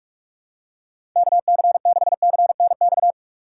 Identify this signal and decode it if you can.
Morse code, heard as a keyed tone, about 45 wpm, 700 Hz. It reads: KC6CNX
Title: USA Call Signs - 6 Characters - 2x Character Spacing - 45wpm